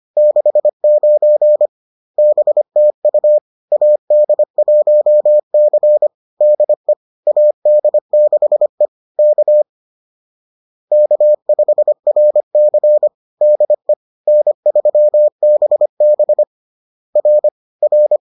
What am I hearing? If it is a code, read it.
Morse code, heard as a keyed tone, about 25 wpm, 600 Hz. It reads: 69 BTU AD1C DE AD6E K K5RC DE N3BB R R